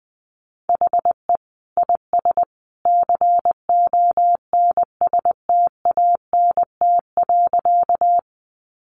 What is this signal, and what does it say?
Morse code, heard as a keyed tone, about 20 wpm, 700 Hz. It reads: HE IS CONSTANT.
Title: Sentences from Top 1500 Words 20wpm